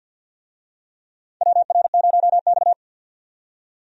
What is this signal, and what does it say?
Morse code, heard as a keyed tone, about 52 wpm, 700 Hz. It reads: WR0X
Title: USA Call Signs - 4 Characters - 50wpm